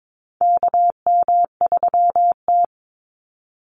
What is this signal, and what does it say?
Morse code, heard as a keyed tone, about 22 wpm, 700 Hz. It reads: KM3T
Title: Call Signs Repeated 2x - 22wpm